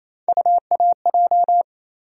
Morse code, heard as a keyed tone, about 28 wpm, 700 Hz. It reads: UAJ
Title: USA Call Sign Postfixes - 3 Letters 28wpm